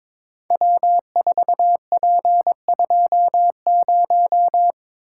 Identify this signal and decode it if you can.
Morse code, heard as a keyed tone, about 22 wpm, 700 Hz. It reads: W4P20
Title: Code Groups - 5 Letter:Numbers - 22wpm